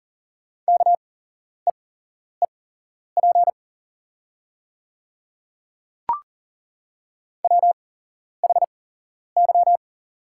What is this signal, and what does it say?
Morse code, heard as a keyed tone, about 40 wpm, spaced as for 11 wpm, 700 Hz. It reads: KEEP WHY
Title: 8x Character Spacing - Top 300 Words - 40wpm